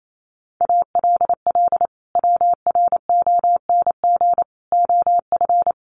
Morse code, heard as a keyed tone, about 28 wpm, 700 Hz. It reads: ALL WRONG OF
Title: Sentences from Top 900 Words 28wpm